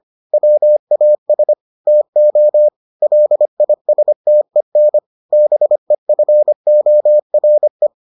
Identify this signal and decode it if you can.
Morse code, heard as a keyed tone, about 25 wpm, 600 Hz. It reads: WAS TO LISTEN BEFORE